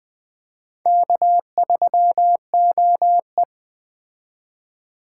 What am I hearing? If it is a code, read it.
Morse code, heard as a keyed tone, about 20 wpm, 700 Hz. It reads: K3OE
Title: Speed Racing - USA Call Signs 17wpm